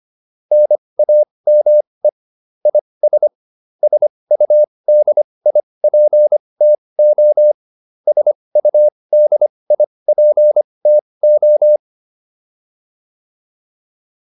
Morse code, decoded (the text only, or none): NAME IS SUDIPTO SUDIPTO